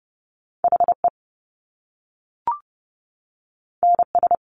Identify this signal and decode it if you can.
Morse code, heard as a keyed tone, about 30 wpm, 700 Hz. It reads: HE NS
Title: Top Two Letter Combinations in Words 30wpm